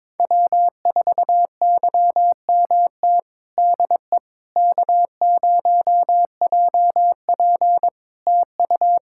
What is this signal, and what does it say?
Morse code, heard as a keyed tone, about 22 wpm, 700 Hz. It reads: W4YMT DE K0JP TU
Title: Practice QSO - Exchange 4 - Ending - 22wpm